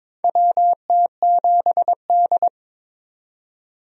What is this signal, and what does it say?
Morse code, heard as a keyed tone, about 22 wpm, 700 Hz. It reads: WT7D
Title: USA Call Signs - 4 Characters - 22wpm